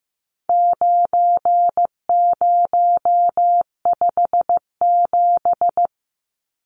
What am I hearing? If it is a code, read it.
Morse code, heard as a keyed tone, about 15 wpm, 700 Hz. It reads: 9057
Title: Numbers - 4-Digit 15wpm